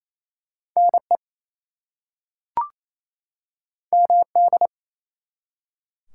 Morse code, heard as a keyed tone, about 29 wpm, 700 Hz. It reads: NE MD
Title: US States and Canadian Provinces 28wpm